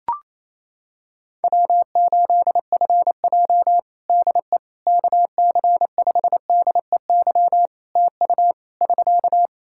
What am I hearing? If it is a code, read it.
Morse code, heard as a keyed tone, about 28 wpm, 700 Hz. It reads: W8FJ DE KC5DEY TU <SK>
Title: Practice QSO - Exchange 4 - Ending - 28wpm